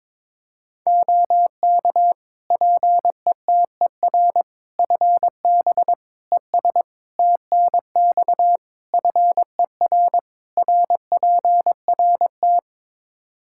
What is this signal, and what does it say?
Morse code, heard as a keyed tone, about 22 wpm, 700 Hz. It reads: OK PETER FB ES TNX FER RPRT
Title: K1USN SST QSO Practice 22wpm